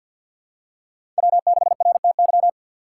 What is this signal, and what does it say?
Morse code, heard as a keyed tone, about 51 wpm, 700 Hz. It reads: W6RTY
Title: USA Call Signs - 5 Characters - 3x Character Spacing - 50wpm